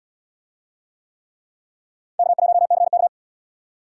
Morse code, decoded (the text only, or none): K1CK